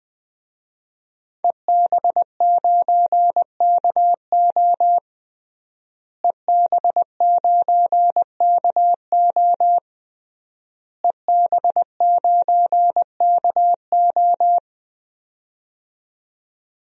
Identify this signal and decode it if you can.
Morse code, heard as a keyed tone, about 20 wpm, 700 Hz. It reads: EB9KO EB9KO EB9KO
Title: DXCC-100 Call Signs - 5 Characters - Repeated 3x - 20wpm